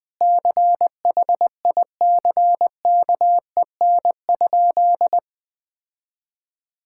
Morse code, heard as a keyed tone, about 20 wpm, 700 Hz. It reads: CHICKEN?
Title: Sentences from Top 1000 Words - Review Part 5 - 20wpm